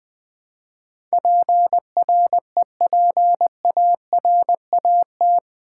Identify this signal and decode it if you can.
Morse code, heard as a keyed tone, about 20 wpm, 700 Hz. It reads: PREPARAT
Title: Build A Word - 4 Letters At A Time - 11-Letter Words - 20wpm